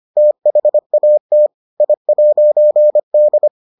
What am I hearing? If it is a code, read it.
Morse code, heard as a keyed tone, about 25 wpm, 600 Hz. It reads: THAT I'D